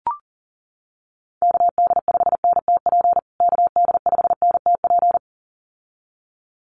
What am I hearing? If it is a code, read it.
Morse code, heard as a keyed tone, about 40 wpm, 700 Hz. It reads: KD5NTP KD5NTP
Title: USA Call Signs Repeated 2x 40wpm